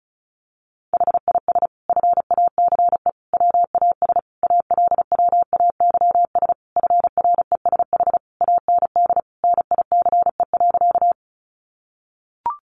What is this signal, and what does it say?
Morse code, heard as a keyed tone, about 34 wpm, 700 Hz. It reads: HIS FACE WAS ALWAYS FRESH AND NICE.
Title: Sentences from Top 1200 Words 35wpm